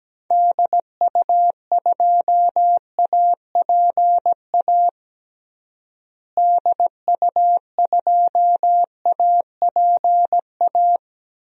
Morse code, heard as a keyed tone, about 17 wpm, 700 Hz. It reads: DU2APA DU2APA
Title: DXCC-100 Call Signs - 6 Characters - Repeated 3x - 17wpm